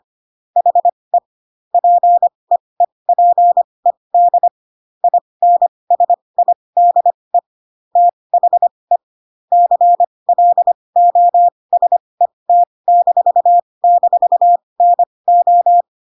Morse code, heard as a keyed tone, about 25 wpm, spaced as for 20 wpm, 700 Hz. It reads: HE PEEPED INSIDE THE CLOSET--NO
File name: Pinocchio / Pinocchio ch1 15 25 ik1hge